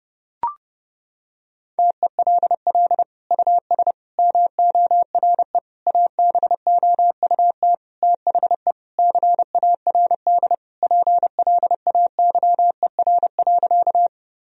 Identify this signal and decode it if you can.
Morse code, heard as a keyed tone, about 32 wpm, 700 Hz. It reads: TELL US MORE ABOUT THE CARD PLAYER.
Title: Sentences from Top 1000 Words - Review Part 4 - 30wpm